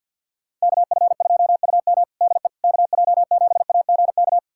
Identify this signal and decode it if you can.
Morse code, heard as a keyed tone, about 53 wpm, 700 Hz. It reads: KF2UM DE KJ7AGK
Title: USA Callsign DE Callsign 50wpm